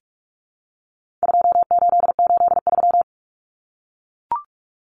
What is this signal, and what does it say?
Morse code, heard as a keyed tone, about 44 wpm, 700 Hz. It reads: 2883
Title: Common Cut Numbers - 4-Digit 45wpm